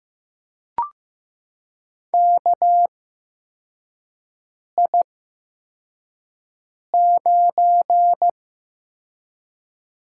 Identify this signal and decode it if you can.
Morse code, heard as a keyed tone, about 15 wpm, spaced as for 4 wpm, 700 Hz. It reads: KI9
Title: USA Call Signs - 4 Characters - 8x Character Spacing - 15wpm